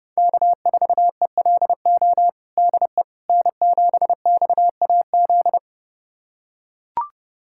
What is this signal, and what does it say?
Morse code, heard as a keyed tone, about 32 wpm, 700 Hz. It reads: K4ELO DE N7XAZ
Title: USA Callsign DE Callsign 30wpm